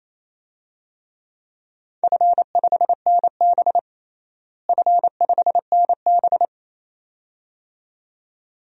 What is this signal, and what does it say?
Morse code, heard as a keyed tone, about 28 wpm, 700 Hz. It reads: F5NB F5NB
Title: DXCC-100 Call Signs - 4 Characters - Repeated 2x - 28wpm